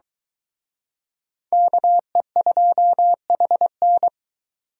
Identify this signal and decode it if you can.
Morse code, heard as a keyed tone, about 22 wpm, 700 Hz. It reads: KE2HN